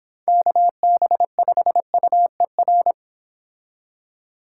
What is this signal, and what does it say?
Morse code, heard as a keyed tone, about 26 wpm, 700 Hz. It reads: KB5UER